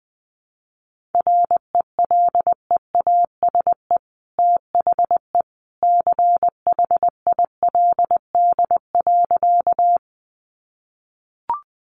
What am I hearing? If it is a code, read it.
Morse code, heard as a keyed tone, about 20 wpm, 700 Hz. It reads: RELEASE THE CHILD.
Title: Sentences from Top 1100 Words 20wpm